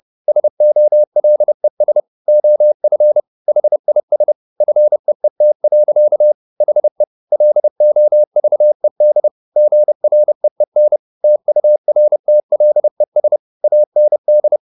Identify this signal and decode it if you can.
Morse code, heard as a keyed tone, about 30 wpm, 600 Hz. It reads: SOLES OF HIS FEET. HE LOVED GREEN TURTLES AND
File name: the old man and the sea / the old man and the sea 1 30wpm